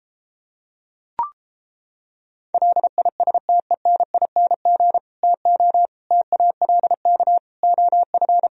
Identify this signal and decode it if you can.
Morse code, heard as a keyed tone, about 29 wpm, 700 Hz. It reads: LISTENING TO TALK OF